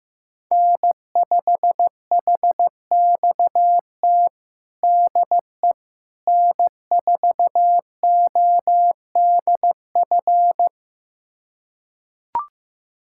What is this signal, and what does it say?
Morse code, heard as a keyed tone, about 15 wpm, 700 Hz. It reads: N5HXT DE N4ODF
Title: USA Callsign DE Callsign - 5 Characters - 15wpm